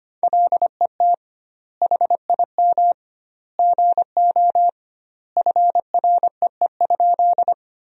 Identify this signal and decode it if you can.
Morse code, heard as a keyed tone, about 25 wpm, 700 Hz. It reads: LET HIM GO FREE?
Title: 2x Word-Spacing - Sentences from Top 500 Words - 25wpm